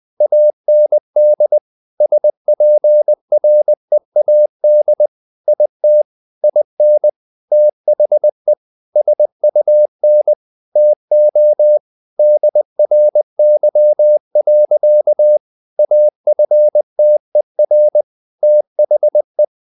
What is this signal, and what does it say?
Morse code, heard as a keyed tone, about 20 wpm, 600 Hz. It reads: AND SPREAD IT IN THE SUN TO DRY. AFTER THE
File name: the old man and the sea / the old man and the sea part 1 20wpm